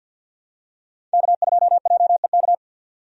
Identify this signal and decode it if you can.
Morse code, heard as a keyed tone, about 50 wpm, 700 Hz. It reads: K2JEK
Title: USA Call Signs - 5 Characters - 50wpm